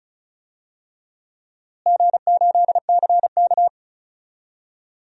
Morse code, heard as a keyed tone, about 36 wpm, 700 Hz. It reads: G8CK
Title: DXCC-100 Call Signs - 4 Characters - Repeated 3x - 35wpm